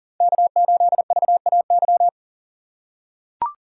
Morse code, heard as a keyed tone, about 40 wpm, 700 Hz. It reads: K8VAY
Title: USA Call Signs - 6x Character Spacing - 40wpm